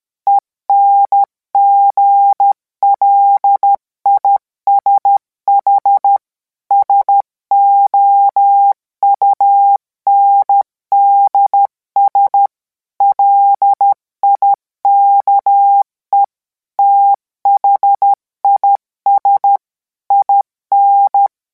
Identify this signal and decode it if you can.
Morse code, heard as a keyed tone, about 13 wpm, 800 Hz. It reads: ENGLISH SOUNDS LIKE THIS IN